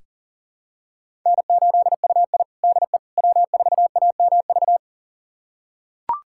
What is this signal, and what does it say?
Morse code, heard as a keyed tone, about 40 wpm, 700 Hz. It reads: N8UI DE W4AMV
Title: USA Callsign DE Callsign 40wpm